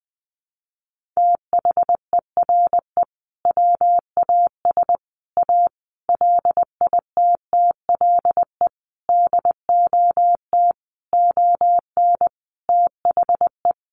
Morse code, heard as a keyed tone, about 20 wpm, 700 Hz. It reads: THERE WAS A LITTLE DOT ON THE
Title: Sentences from Top 1500 Words 20wpm